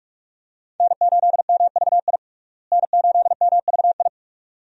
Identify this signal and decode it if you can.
Morse code, heard as a keyed tone, about 46 wpm, 700 Hz. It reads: N8MVI N8MVI
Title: USA Call Signs - 5 Characters - Repeated 2x - 45wpm